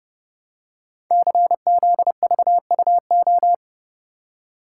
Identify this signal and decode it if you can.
Morse code, heard as a keyed tone, about 28 wpm, 700 Hz. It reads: CZVUO
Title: Code Groups - 5 Letter:Numbers - 30wpm